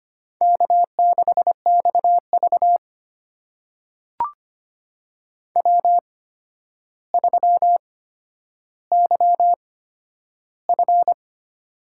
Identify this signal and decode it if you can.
Morse code, heard as a keyed tone, about 25 wpm, spaced as for 11 wpm, 700 Hz. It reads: K6XV W3YF
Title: USA Call Signs - 4 Characters - 8x Character Spacing - 25wpm